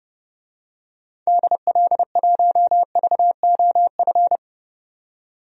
Morse code, heard as a keyed tone, about 30 wpm, 700 Hz. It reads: DL1VOF
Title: DXCC-100 Call Signs - 6 Characters - 30wpm